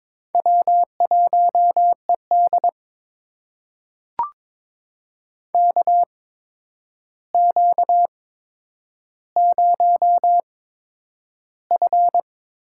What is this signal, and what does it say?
Morse code, heard as a keyed tone, about 21 wpm, spaced as for 9 wpm, 700 Hz. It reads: W1ED KQ0F
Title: USA Call Signs - 4 Characters - 8x Character Spacing - 22wpm